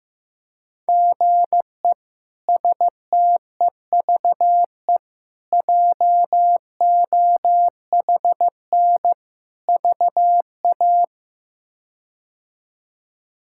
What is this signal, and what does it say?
Morse code, heard as a keyed tone, about 15 wpm, 700 Hz. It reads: GE STEVE JOHN VA
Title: K1USN SST QSO Practice - 15wpm